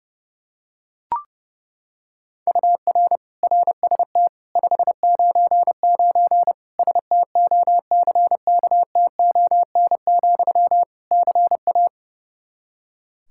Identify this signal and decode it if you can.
Morse code, heard as a keyed tone, about 30 wpm, 700 Hz. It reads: UR RST 599 STOCKTON, CA
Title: K1USN SST QSO Practice 30wpm